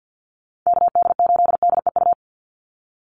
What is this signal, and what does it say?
Morse code, heard as a keyed tone, about 50 wpm, 700 Hz. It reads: KD8DEU
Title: NA SOTA Chaser Callsigns 50wpm